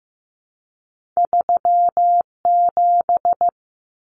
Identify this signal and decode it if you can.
Morse code, heard as a keyed tone, about 15 wpm, 700 Hz. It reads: 37
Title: Numbers - 2-Digit 15wpm